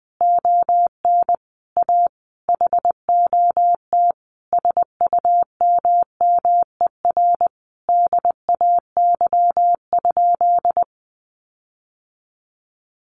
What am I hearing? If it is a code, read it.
Morse code, heard as a keyed tone, about 20 wpm, 700 Hz. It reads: ON A HOT SUMMER DAY?
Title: Sentences from Top 1500 Words 20wpm